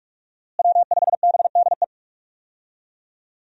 Morse code, heard as a keyed tone, about 46 wpm, 700 Hz. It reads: W5BDE
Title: USA Call Signs - 5 Characters - Repeated 2x - 45wpm